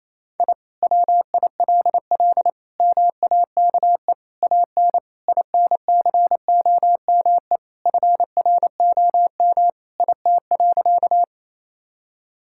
Morse code, heard as a keyed tone, about 28 wpm, 700 Hz. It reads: I WILL MAKE AN INCOME FROM IT.
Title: Sentences from Top 1000 Words - Review Part 5 - 28wpm